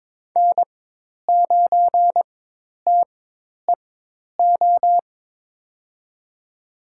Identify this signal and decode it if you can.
Morse code, heard as a keyed tone, about 23 wpm, spaced as for 10 wpm, 700 Hz. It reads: N9TEO
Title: USA Call Signs - 5 Characters - 4x Character Spacing - 22wpm